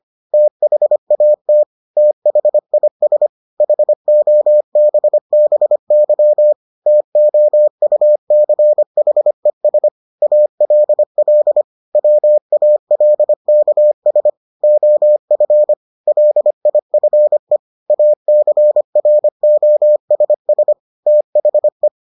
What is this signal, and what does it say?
Morse code, heard as a keyed tone, about 25 wpm, 600 Hz. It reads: THAT THIS HOBBY TOUCHES ALL WALKS OF LIFE ACROSS THE